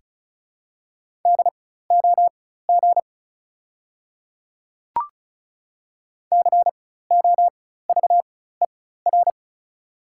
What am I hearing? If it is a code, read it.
Morse code, heard as a keyed tone, about 36 wpm, spaced as for 16 wpm, 700 Hz. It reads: DOG COVER
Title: 4x Character Spacing - Top 500 Words - 35wpm